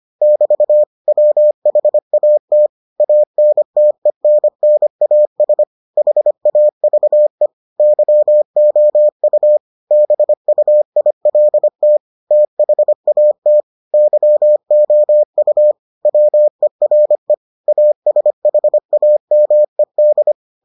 Morse code, heard as a keyed tone, about 25 wpm, 600 Hz. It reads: <BT> WHAT ANTENNAS HAVE YOU BUILT THAT YOU WERE ASHAMED